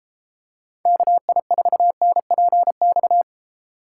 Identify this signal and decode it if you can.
Morse code, heard as a keyed tone, about 32 wpm, 700 Hz. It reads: KI4NPX